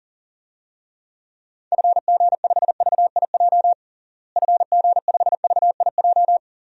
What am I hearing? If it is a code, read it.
Morse code, heard as a keyed tone, about 40 wpm, 700 Hz. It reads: FG5VIJ FG5VIJ
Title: DXCC-100 Call Signs - 6 Characters - Repeated 2x - 40wpm